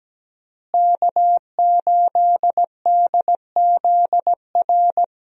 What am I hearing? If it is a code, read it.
Morse code, heard as a keyed tone, about 17 wpm, 700 Hz. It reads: K8DZR
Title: USA Call Signs - 5 Characters - Repeated 2x - 17wpm